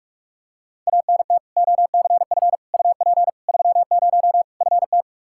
Morse code, heard as a keyed tone, about 46 wpm, 700 Hz. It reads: ANT OCF UP 30 FT
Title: Practice QSO - Exchange 2 - Info - 45wpm